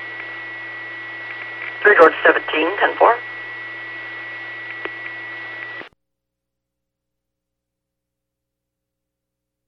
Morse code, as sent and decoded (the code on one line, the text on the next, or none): none